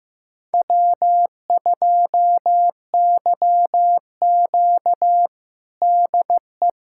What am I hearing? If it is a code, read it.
Morse code, heard as a keyed tone, about 15 wpm, 700 Hz. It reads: W2YQ DE
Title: USA Callsign DE Callsign - 4 Characters - 15wpm